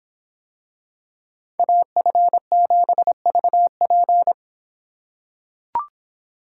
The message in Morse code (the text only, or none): AF7VP